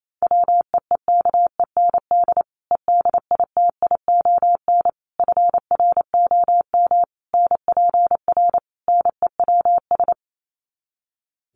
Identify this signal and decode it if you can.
Morse code, heard as a keyed tone, about 28 wpm, 700 Hz. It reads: WEEKEND EDITION FROM NPR NEWS
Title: Sets of 5 Words 28wpm